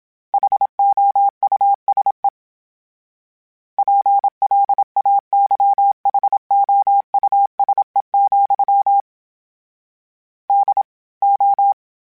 Morse code, heard as a keyed tone, about 27 wpm, 800 Hz. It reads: HOUSE PLAYHOUSE, DO
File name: Houses @25